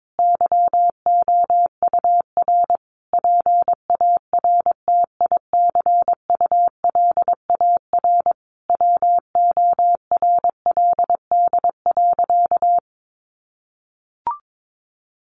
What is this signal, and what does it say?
Morse code, heard as a keyed tone, about 22 wpm, 700 Hz. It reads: YOUR PARTICULAR WORLD.
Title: Sentences from Top 800 Words 22wpm